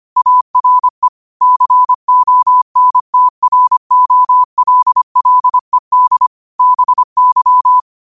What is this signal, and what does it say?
Morse code, heard as a keyed tone, about 25 wpm, 1000 Hz. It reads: ARE CONTROLLED BY
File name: qotdfiles / 25wpm